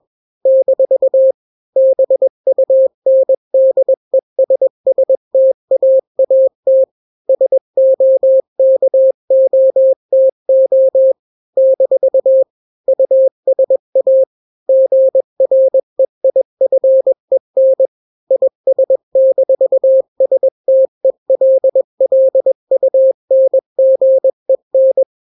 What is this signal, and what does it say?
Morse code, heard as a keyed tone, about 21 wpm, 530 Hz. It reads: - BUNDESSTAAT SOKOTO - USA GREIFEN IS-STELLUNGEN
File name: daily / 20wpm